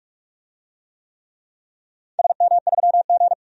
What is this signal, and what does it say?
Morse code, heard as a keyed tone, about 46 wpm, 700 Hz. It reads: SM3G
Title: DXCC-100 Call Signs - 4 Characters - 45wpm